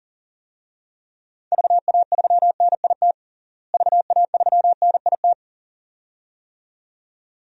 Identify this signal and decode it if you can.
Morse code, heard as a keyed tone, about 37 wpm, 700 Hz. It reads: VA3NIT VA3NIT
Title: DXCC-100 Call Signs - 6 Characters - Repeated 2x - 40wpm